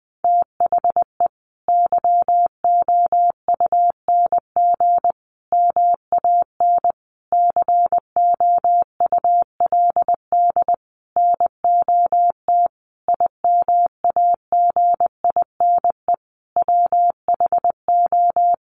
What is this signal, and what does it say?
Morse code, heard as a keyed tone, about 20 wpm, 700 Hz. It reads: THE YOUNG MAN COULD NOT IMAGINE WHO
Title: Sentences from Top 1400 Words 20wpm